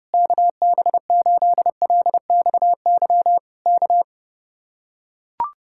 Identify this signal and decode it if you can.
Morse code, heard as a keyed tone, about 32 wpm, 700 Hz. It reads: KB8LXY K